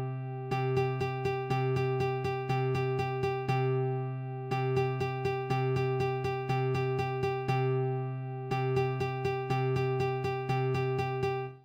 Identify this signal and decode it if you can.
There is no code.